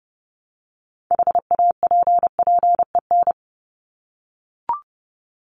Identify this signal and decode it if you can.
Morse code, heard as a keyed tone, about 30 wpm, 700 Hz. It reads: HAPPEN
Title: ED Suffix - Top 100 Words with Base Words 30wpm